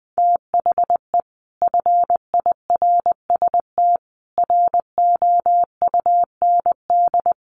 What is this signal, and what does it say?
Morse code, heard as a keyed tone, about 19 wpm, 700 Hz. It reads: THE FIRST ROUND
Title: Sets of 5 Words Encore 20wpm